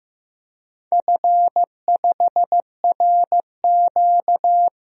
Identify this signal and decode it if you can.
Morse code, heard as a keyed tone, about 15 wpm, 700 Hz. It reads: F5RQ
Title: DXCC-100 Call Signs - 4 Characters - Repeated 3x - 15wpm